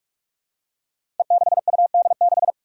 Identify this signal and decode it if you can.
Morse code, heard as a keyed tone, about 41 wpm, 700 Hz. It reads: E6UD6
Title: Five Challenging Letter-Number Combinations 45wpm